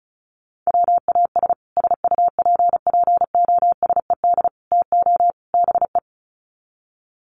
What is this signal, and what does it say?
Morse code, heard as a keyed tone, about 34 wpm, 700 Hz. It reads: WAS SUPPOSED TO BE
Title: Sets of 4 Words 35wpm